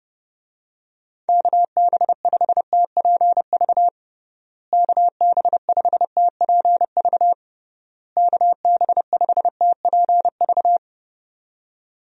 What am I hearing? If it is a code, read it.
Morse code, heard as a keyed tone, about 30 wpm, 700 Hz. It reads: KB5TPV KB5TPV KB5TPV
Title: USA Call Signs - 6 Characters - Repeated 3x - 30wpm